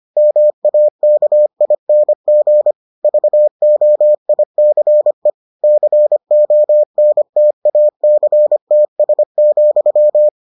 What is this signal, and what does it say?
Morse code, heard as a keyed tone, about 25 wpm, 600 Hz. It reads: MAKING VOICE CONTACTS,